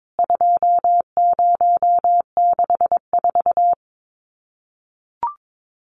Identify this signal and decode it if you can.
Morse code, heard as a keyed tone, about 22 wpm, 700 Hz. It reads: 2064
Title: CWT Exchange 22wpm